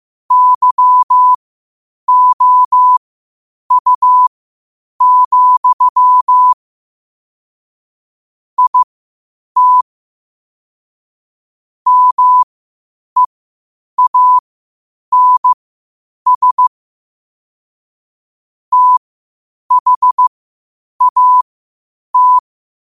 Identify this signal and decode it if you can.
Morse code, heard as a keyed tone, about 15 wpm, spaced as for 8 wpm, 1000 Hz. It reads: YOU, IT MEANS THAT
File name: qotdfiles / 8wpm